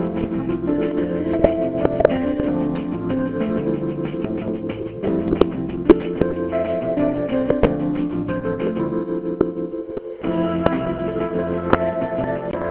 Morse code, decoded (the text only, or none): none